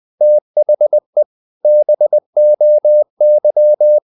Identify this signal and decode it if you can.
Morse code, heard as a keyed tone, about 20 wpm, 600 Hz. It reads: THE BOY
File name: the old man and the sea / the old man and the sea part 1 20wpm